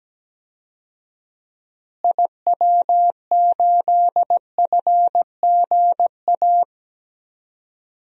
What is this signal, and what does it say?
Morse code, heard as a keyed tone, about 17 wpm, 700 Hz. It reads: IW8FGA
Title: DXCC-100 Call Signs - 6 Characters - Repeated 3x - 17wpm